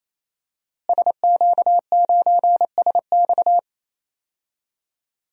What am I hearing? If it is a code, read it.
Morse code, heard as a keyed tone, about 27 wpm, 700 Hz. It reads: SQ9SX
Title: DXCC-100 Call Signs - 5 Characters - Repeated 3x - 28wpm